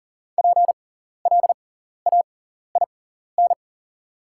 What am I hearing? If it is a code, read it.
Morse code, heard as a keyed tone, about 40 wpm, spaced as for 14 wpm, 690 Hz. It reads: PLAIN
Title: 6x Character Spacing - Top 500 Words - 40wpm